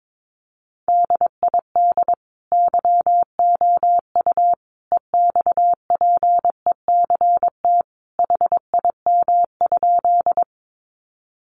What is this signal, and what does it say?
Morse code, heard as a keyed tone, about 22 wpm, 700 Hz. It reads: DID YOU EXPECT HIM?
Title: Sentences from Top 1100 Words 22wpm